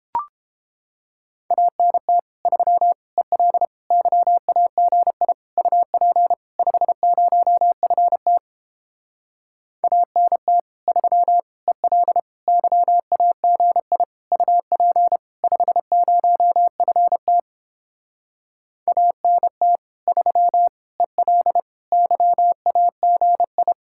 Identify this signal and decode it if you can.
Morse code, heard as a keyed tone, about 29 wpm, 700 Hz. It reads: ANT 3 EL YAGI UP 50FT ANT 3 EL YAGI UP 50FT ANT 3 EL YAGI